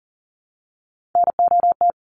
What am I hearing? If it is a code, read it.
Morse code, heard as a keyed tone, about 40 wpm, 700 Hz. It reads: NOT
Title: Sentences from Top 1400 Words 40wpm